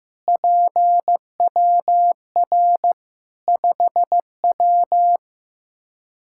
Morse code, heard as a keyed tone, about 15 wpm, 700 Hz. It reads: PWR 5W
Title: Practice QSO - Exchange 2 - Info - 15wpm